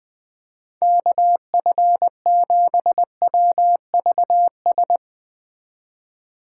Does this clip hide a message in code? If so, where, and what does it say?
Yes: Morse code from 0.8 to 5.0 s, heard as a keyed tone, about 20 wpm, 700 Hz. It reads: KF7WVS